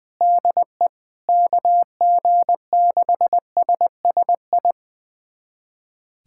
Morse code, heard as a keyed tone, about 20 wpm, 700 Hz. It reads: DE KG6SSI
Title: USA Callsign DE Callsign - 6 Characters - 20wpm